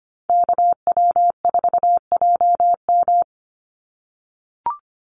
Morse code, heard as a keyed tone, about 25 wpm, 700 Hz. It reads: KW4JM
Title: NA SOTA Activator Callsigns 25wpm